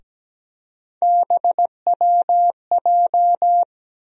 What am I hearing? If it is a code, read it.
Morse code, heard as a keyed tone, about 17 wpm, 700 Hz. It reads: BWJ
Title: USA Call Sign Postfixes - 3 Letters 17wpm